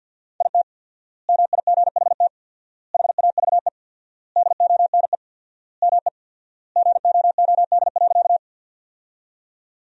Morse code, heard as a keyed tone, about 53 wpm, 700 Hz. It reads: IT MIGHT HAVE DONE ME GOOD.
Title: 4x Word-Spacing - Sentences from Top 500 Words - 50wpm